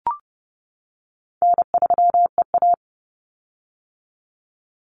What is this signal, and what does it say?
Morse code, heard as a keyed tone, about 30 wpm, 700 Hz. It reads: N3EA
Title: Easy Call Signs - 30wpm